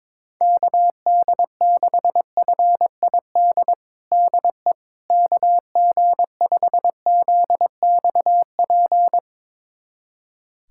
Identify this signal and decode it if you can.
Morse code, heard as a keyed tone, about 22 wpm, 700 Hz. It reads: KD6FID DE KG5ZXP
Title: USA Callsign DE Callsign - 6 Characters - 22wpm